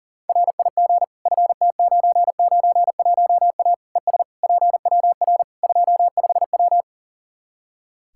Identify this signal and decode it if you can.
Morse code, heard as a keyed tone, about 40 wpm, 700 Hz. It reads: RIG FT991A ES PWR 25W
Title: Practice QSO - Exchange 2 - Info - 40wpm